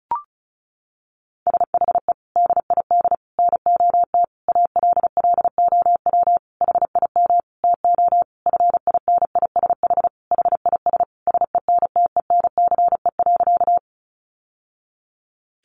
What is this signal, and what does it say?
Morse code, heard as a keyed tone, about 34 wpm, 700 Hz. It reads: SHE DID NOT ALLOW HIM TO FINISH HIS SENTENCE.
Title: Sentences from Top 1200 Words 35wpm